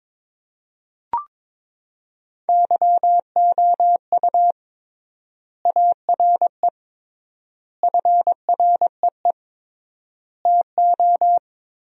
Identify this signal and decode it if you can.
Morse code, heard as a keyed tone, about 22 wpm, 700 Hz. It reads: YOU ARE FREE TO
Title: 3x Word-Spacing - Sentences from Top 500 Words - 22wpm